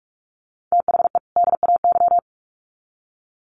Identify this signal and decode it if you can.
Morse code, heard as a keyed tone, about 44 wpm, 700 Hz. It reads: THE DAY
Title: Sets of 2 Words 45wpm